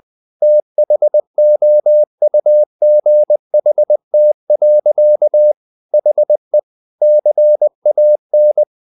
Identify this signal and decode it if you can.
Morse code, heard as a keyed tone, about 20 wpm, 600 Hz. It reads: THOUGHT. HE CAN
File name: the old man and the sea / the old man and the sea part 1 20wpm